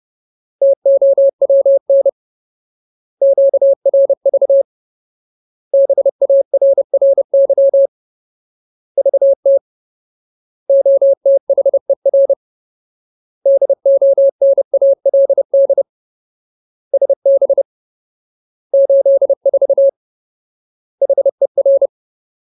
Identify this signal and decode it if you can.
Morse code, heard as a keyed tone, about 30 wpm, 550 Hz. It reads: TOWN QRV BARRY VT OTHER DONALD SB 84 HER
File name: Ultimate Word List / 30WPM Ultimate